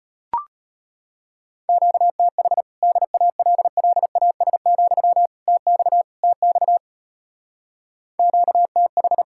Morse code, heard as a keyed tone, about 33 wpm, 700 Hz. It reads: QTH DALLAS, TX TX QTH